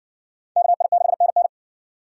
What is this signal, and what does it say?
Morse code, heard as a keyed tone, about 54 wpm, 700 Hz. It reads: KI6NN